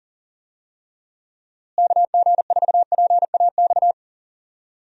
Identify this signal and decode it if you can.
Morse code, heard as a keyed tone, about 40 wpm, 700 Hz. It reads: KG4PAX
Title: USA Call Signs - 6 Characters - Repeated 2x - 40wpm